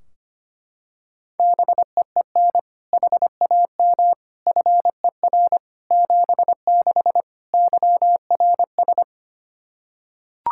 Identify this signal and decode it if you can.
Morse code, heard as a keyed tone, about 25 wpm, 700 Hz. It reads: BEEN HAM FER 76 YRS
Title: Practice QSO - Exchange 3 - More Info - 25wpm